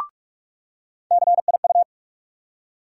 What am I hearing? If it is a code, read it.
Morse code, heard as a keyed tone, about 42 wpm, 700 Hz. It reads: CIU
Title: USA Call Sign Postfixes - 3 Letters 45wpm